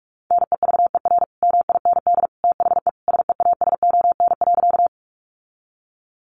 Morse code, heard as a keyed tone, about 44 wpm, 700 Hz. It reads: NEVER MIND THE SEASON.
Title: Sentences from Top 1400 Words 45wpm